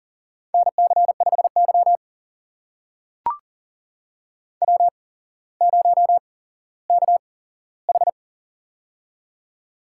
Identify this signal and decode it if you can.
Morse code, heard as a keyed tone, about 42 wpm, spaced as for 17 wpm, 700 Hz. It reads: NC5Y W0KH
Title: USA Call Signs - 8x Character Spacing - 40wpm